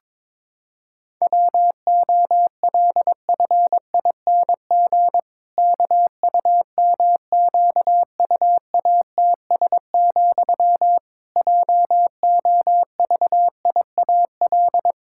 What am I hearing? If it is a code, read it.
Morse code, heard as a keyed tone, about 22 wpm, 700 Hz. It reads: WOLFING KUMQUATS, JOVIAL
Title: Pangram Challenge v2 - 22wpm